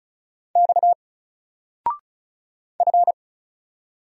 Morse code, heard as a keyed tone, about 35 wpm, 700 Hz. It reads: X F
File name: Single-Letter-Number-35wpm